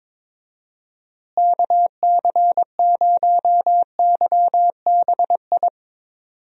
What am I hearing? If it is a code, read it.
Morse code, heard as a keyed tone, about 22 wpm, 700 Hz. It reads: KC0YBI
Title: USA Call Signs - 6 Characters - Repeated 2x - 22wpm